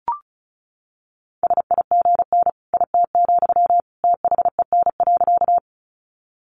Morse code, heard as a keyed tone, about 34 wpm, 700 Hz. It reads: SIGN IT, THEN.
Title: Sentences from Top 1000 Words 35wpm